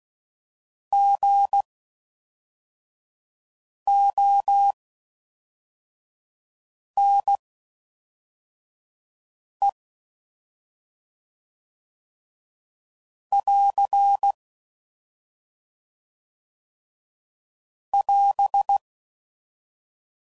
Morse code, heard as a keyed tone, about 16 wpm, spaced as for 4 wpm, 800 Hz. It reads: GONE <AR> <AS>